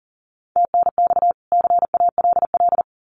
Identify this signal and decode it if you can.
Morse code, heard as a keyed tone, about 40 wpm, 700 Hz. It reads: TNX CALL
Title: Common QSO Phrases 40wpm